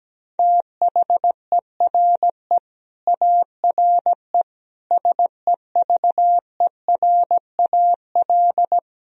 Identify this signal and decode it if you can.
Morse code, heard as a keyed tone, about 17 wpm, 700 Hz. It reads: THERE ARE SEVERAL
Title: Sentences from Top 1000 Words - Review Part 2 - 17wpm